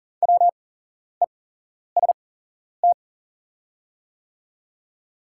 Morse code, heard as a keyed tone, about 40 wpm, spaced as for 11 wpm, 700 Hz. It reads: WEST
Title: 8x Character Spacing - Top 500 Words - 40wpm